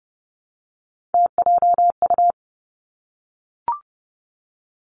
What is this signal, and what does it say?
Morse code, heard as a keyed tone, about 30 wpm, 700 Hz. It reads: TJU
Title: Three Challenging Letter-Number Combinations 30wpm